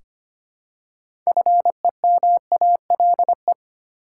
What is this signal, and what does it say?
Morse code, heard as a keyed tone, about 25 wpm, 700 Hz. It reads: FEMALE
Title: Six Letter Words - 25wpm